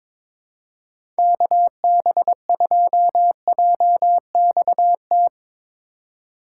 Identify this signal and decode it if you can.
Morse code, heard as a keyed tone, about 22 wpm, 700 Hz. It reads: KB2JXT
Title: USA Call Signs - 6x Character Spacing - 22wpm